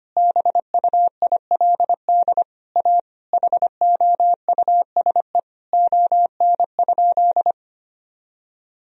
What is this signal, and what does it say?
Morse code, heard as a keyed tone, about 25 wpm, 700 Hz. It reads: BUILD A HOUSE ON?
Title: Sentences from Top 1000 Words - Review Part 4 - 25wpm